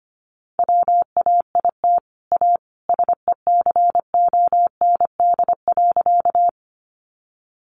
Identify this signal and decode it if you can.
Morse code, heard as a keyed tone, about 25 wpm, 700 Hz. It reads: WAIT A SECOND.
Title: Sentences from Top 500 Words 25wpm